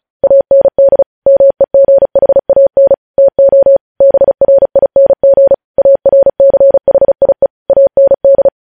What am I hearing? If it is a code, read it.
Morse code, heard as a keyed tone, about 36 wpm, 550 Hz. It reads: AND MEGHAN TO BRING ARCHIE AND
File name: podcast / News headlines-1x-35x35